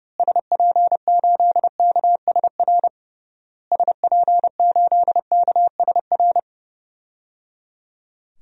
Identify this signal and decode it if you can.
Morse code, heard as a keyed tone, about 29 wpm, 700 Hz. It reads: SP8KSR SP8KSR
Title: DXCC-100 Call Signs - 6 Characters - Repeated 3x - 30wpm